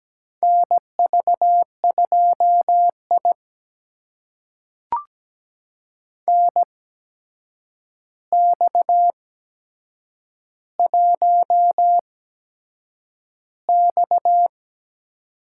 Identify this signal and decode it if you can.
Morse code, heard as a keyed tone, about 17 wpm, spaced as for 7 wpm, 700 Hz. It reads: NV2I NX1X
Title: USA Call Signs - 4 Characters - 8x Character Spacing - 17wpm